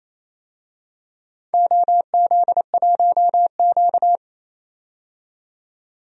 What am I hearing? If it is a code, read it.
Morse code, heard as a keyed tone, about 28 wpm, 700 Hz. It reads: OZ1Q
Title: DXCC-100 Call Signs - 4 Characters - 28wpm